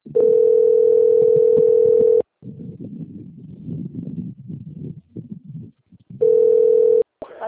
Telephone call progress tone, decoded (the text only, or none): ringback tone